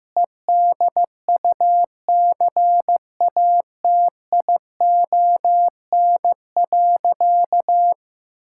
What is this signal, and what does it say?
Morse code, heard as a keyed tone, about 16 wpm, 700 Hz. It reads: EDUCATION.
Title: Sentences from Top 1000 Words - Review Part 6 - 15wpm